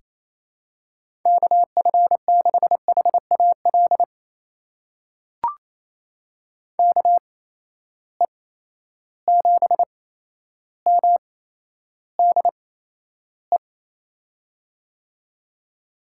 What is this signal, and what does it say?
Morse code, heard as a keyed tone, about 28 wpm, spaced as for 12 wpm, 700 Hz. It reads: KF6HAL KE7MDE
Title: USA Call Signs - 6 Characters - 8x Character Spacing - 28wpm